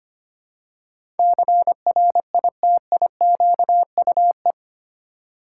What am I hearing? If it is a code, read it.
Morse code, heard as a keyed tone, about 25 wpm, 700 Hz. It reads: CRITIQUE
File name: 8 Letter Words - First 3 Letters Given - 25wpm